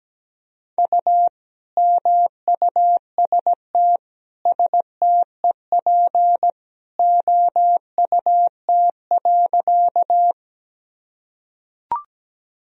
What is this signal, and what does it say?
Morse code, heard as a keyed tone, about 17 wpm, 700 Hz. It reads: U MUST STEP OUT.
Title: Sentences from Top 400 Words and Abbreviations 17wpm